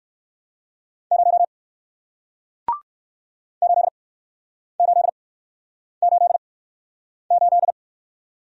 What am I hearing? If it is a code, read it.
Morse code, heard as a keyed tone, about 58 wpm, spaced as for 9 wpm, 700 Hz. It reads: 0 8888